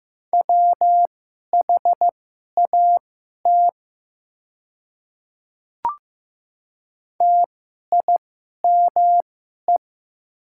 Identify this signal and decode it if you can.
Morse code, heard as a keyed tone, about 15 wpm, spaced as for 11 wpm, 700 Hz. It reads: WHAT TIME
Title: 2x Character Spacing - Top 200 Words - 15wpm